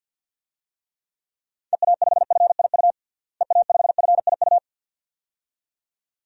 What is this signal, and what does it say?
Morse code, heard as a keyed tone, about 49 wpm, 700 Hz. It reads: EA5FIU EA5FIU
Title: DXCC-100 Call Signs - 6 Characters - Repeated 2x - 50wpm